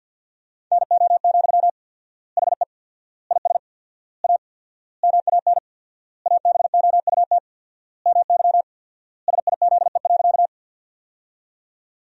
Morse code, heard as a keyed tone, about 50 wpm, 700 Hz. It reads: NO, HE IS A MAN ABOUT MY SIZE.
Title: 4x Word-Spacing - Sentences from Top 400 Words - 50wpm